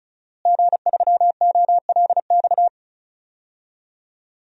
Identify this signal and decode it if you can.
Morse code, heard as a keyed tone, about 34 wpm, 690 Hz. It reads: G3OLX
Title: DXCC-100 Call Signs - 5 Characters - Repeated 3x - 35wpm